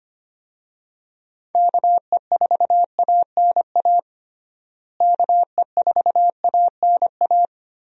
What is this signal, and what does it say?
Morse code, heard as a keyed tone, about 25 wpm, 700 Hz. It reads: KE4ANA KE4ANA
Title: USA Call Signs - 6 Characters - Repeated 3x - 25wpm